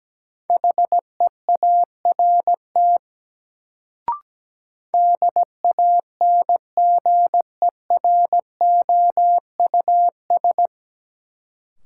Morse code, heard as a keyed tone, about 17 wpm, 700 Hz. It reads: HEART DANGEROUS
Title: Sentences from Top 1200 Words - New Word Drill - 17wpm